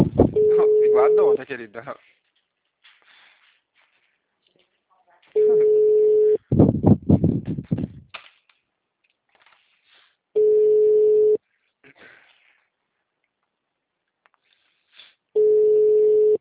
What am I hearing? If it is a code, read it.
Ringback tone (Europe, 425 Hz), with speech.
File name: out- -2038- - - .3181